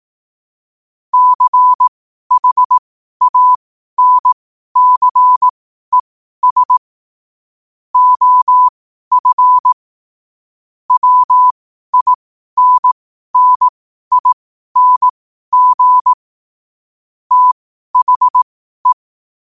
Morse code, heard as a keyed tone, about 18 wpm, spaced as for 12 wpm, 1000 Hz. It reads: CHANCES OF WINNING THE
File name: 12wpm